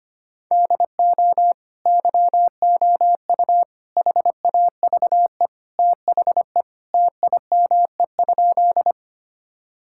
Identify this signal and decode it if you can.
Morse code, heard as a keyed tone, about 25 wpm, 700 Hz. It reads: DO YOU HAVE THE TIME?
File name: practice / 25wpm / Sentences with 2-Word Phrases - 25wpm